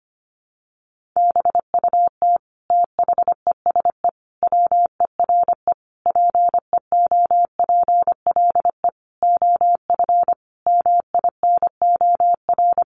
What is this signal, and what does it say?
Morse code, heard as a keyed tone, about 25 wpm, 700 Hz. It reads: BUT THESE WERE PEOPLE OF MINOR
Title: Sentences from Top 1500 Words 25wpm